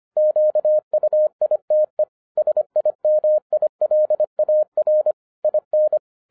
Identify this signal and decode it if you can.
Morse code, heard as a keyed tone, about 26 wpm, 610 Hz. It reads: QUITE SIMILAR IN
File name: 2012-01-mp3 / n14-FUN WITH NUMBERS-25wpm